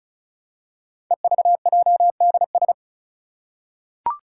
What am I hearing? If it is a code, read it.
Morse code, heard as a keyed tone, about 36 wpm, 700 Hz. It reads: EVJDS
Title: Code Groups - 5 Letters - 35wpm